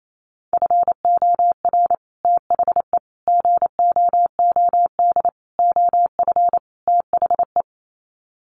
Morse code, heard as a keyed tone, about 28 wpm, 700 Hz. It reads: FOR THE GOOD OF THE
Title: Sets of 5 Words Encore2 28wpm